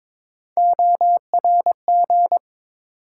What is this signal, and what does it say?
Morse code, heard as a keyed tone, about 22 wpm, 700 Hz. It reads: ORG
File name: ARRL Sections v2 - 22wpm